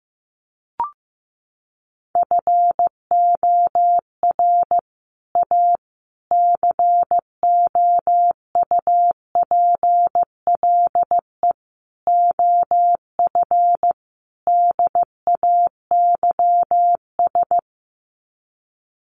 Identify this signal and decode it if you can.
Morse code, heard as a keyed tone, about 15 wpm, 700 Hz. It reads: FOR A COUPLE OF DAYS
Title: Sets of 5 Words Encore2 15wpm